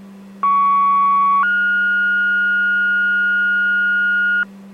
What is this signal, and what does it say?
Two-tone page, heard as a one-second tone, then a three-second tone, about 1120 then 1430 Hz, with noise. It is heard from 0.4 to 4.4 s.